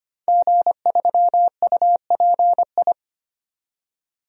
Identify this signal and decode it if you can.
Morse code, heard as a keyed tone, about 25 wpm, 700 Hz. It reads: G3UPI
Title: DXCC-100 Call Signs - 5 Characters - 25wpm